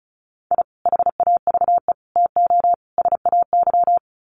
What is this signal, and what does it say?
Morse code, heard as a keyed tone, about 34 wpm, 700 Hz. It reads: I HAVE TO SAY
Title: Sets of 4 Words 35wpm